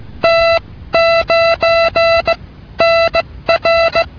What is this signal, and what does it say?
Morse code, heard as a keyed tone, about 16 wpm, 670 Hz. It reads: T 9 N R